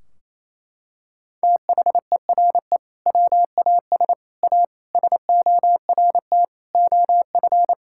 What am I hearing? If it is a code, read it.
Morse code, heard as a keyed tone, about 28 wpm, 700 Hz. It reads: THERE WAS A SORT OF
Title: Sentences from Top 1600 Words 28wpm